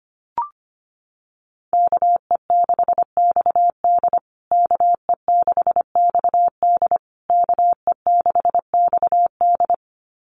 Morse code, heard as a keyed tone, about 25 wpm, 700 Hz. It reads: KE6XD KE6XD KE6XD
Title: USA Call Signs Repeated 3x 25wpm